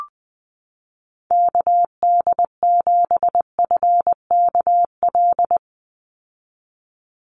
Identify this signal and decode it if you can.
Morse code, heard as a keyed tone, about 20 wpm, 700 Hz. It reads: KD7FKL
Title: USA Call Signs 20wpm